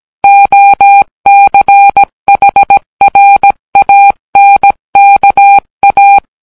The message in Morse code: OCHRANKA